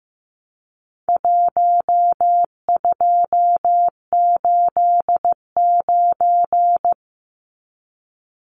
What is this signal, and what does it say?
Morse code, heard as a keyed tone, about 15 wpm, 700 Hz. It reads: 1289
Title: Numbers - Contest Style - 4-Digit 15wpm